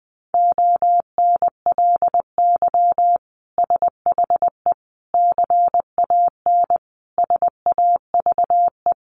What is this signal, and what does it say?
Morse code, heard as a keyed tone, about 20 wpm, 700 Hz. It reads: ONLY SHE CAN SAVE
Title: Sentences from Top 1100 Words 20wpm